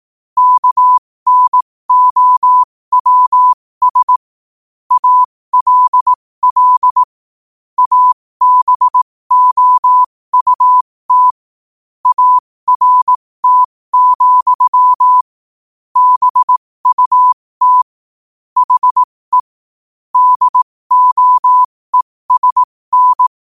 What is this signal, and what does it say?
Morse code, heard as a keyed tone, about 18 wpm, spaced as for 15 wpm, 1000 Hz. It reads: KNOWS ALL ABOUT ART, BUT HE DOESN
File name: qotdfiles / 15wpm